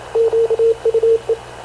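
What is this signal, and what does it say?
Morse code, heard as a keyed tone, about 28 wpm, 450 Hz, with radio noise. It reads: QUE
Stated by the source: Havana Radio / CLA sending Spanish press (PX) in Morse, recorded by Radio Officer Paul Dumesnil onboard CNAV QUEST/CZDO.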